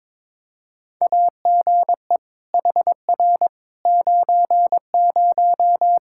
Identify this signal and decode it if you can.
Morse code, heard as a keyed tone, about 21 wpm, 700 Hz. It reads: AGE HR 90
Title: Practice QSO - Exchange 3 - More Info - 22wpm